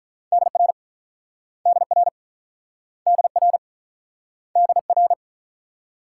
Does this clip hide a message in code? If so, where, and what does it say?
Yes: Morse code from 0.3 to 5.1 s, heard as a keyed tone, about 44 wpm, 700 Hz. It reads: DR DR DR DR